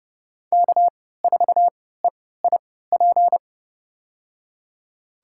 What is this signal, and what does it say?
Morse code, heard as a keyed tone, about 30 wpm, spaced as for 17 wpm, 700 Hz. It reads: K4EIP
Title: USA Call Signs - 5 Characters - 3x Character Spacing - 30wpm